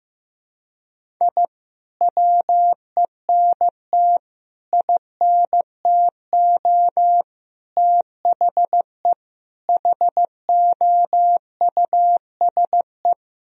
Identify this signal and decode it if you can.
Morse code, heard as a keyed tone, about 15 wpm, 700 Hz. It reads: I WENT INTO THE HOUSE
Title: Sentences from Top 1000 Words - Review Part 1 - 15wpm